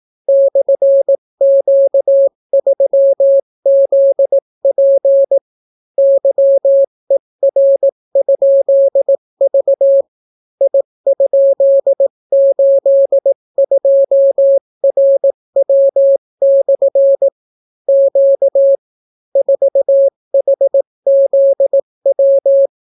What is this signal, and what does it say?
Morse code, heard as a keyed tone, about 18 wpm, 550 Hz. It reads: /Q3ZP YER?V I?82RW/ Q 4HZW